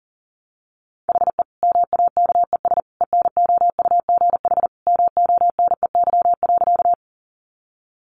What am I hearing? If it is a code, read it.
Morse code, heard as a keyed tone, about 40 wpm, 700 Hz. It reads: HE MAKES ENOUGH MONEY.
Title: Additional Easy Phrases 40wpm